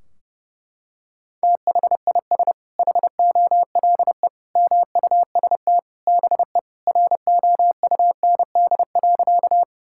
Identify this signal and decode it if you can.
Morse code, heard as a keyed tone, about 32 wpm, 700 Hz. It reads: THIS HOLE MUST BE ROUND.
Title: Sentences from Top 1600 Words 30wpm